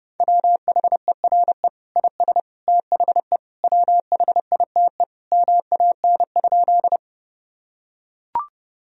Morse code, heard as a keyed tone, about 32 wpm, 700 Hz. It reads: WHERE IS THE WHITE MAN?
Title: Sentences from Top 1000 Words - Review Part 1 - 30wpm